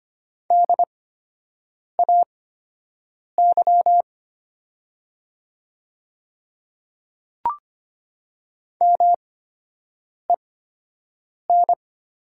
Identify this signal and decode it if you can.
Morse code, heard as a keyed tone, about 25 wpm, spaced as for 7 wpm, 700 Hz. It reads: DAY MEN
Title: 8x Character Spacing - Top 300 Words - 25wpm